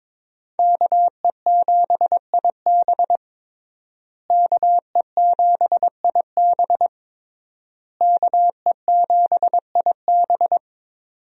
Morse code, heard as a keyed tone, about 22 wpm, 700 Hz. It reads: KE7IB KE7IB KE7IB
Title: USA Call Signs - 5 Characters - Repeated 3x - 22wpm